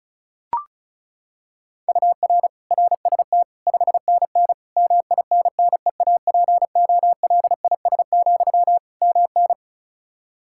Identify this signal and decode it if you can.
Morse code, heard as a keyed tone, about 36 wpm, 700 Hz. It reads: UR RST 5NN MINNEAPOLIS, MN
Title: Practice QSO - Exchange 1 - Beginning - 35wpm